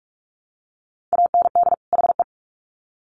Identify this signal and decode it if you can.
Morse code, heard as a keyed tone, about 44 wpm, 700 Hz. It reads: AND HE
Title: Sets of 2 Words 45wpm